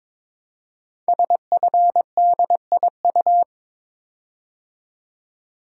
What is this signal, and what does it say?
Morse code, heard as a keyed tone, about 23 wpm, 700 Hz. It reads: SFDIU
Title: Code Groups - 5 Letters - 22wpm